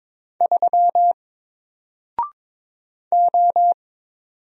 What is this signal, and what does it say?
Morse code, heard as a keyed tone, about 22 wpm, 700 Hz. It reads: 3 O
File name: Single-Letter-Number-22wpm